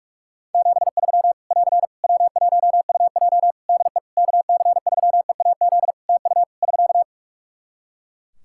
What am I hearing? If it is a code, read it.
Morse code, heard as a keyed tone, about 46 wpm, 700 Hz. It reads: 73 <AR> W1UJ DE KC3EAZ TU <SK>
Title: Practice QSO - Exchange 4 - Ending - 45wpm